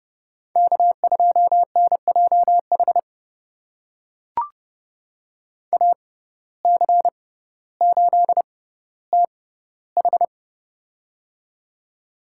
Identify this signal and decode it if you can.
Morse code, heard as a keyed tone, about 30 wpm, spaced as for 15 wpm, 700 Hz. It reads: K2NJH AC8TH
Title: USA Call Signs - 5 Characters - 6x Character Spacing - 30wpm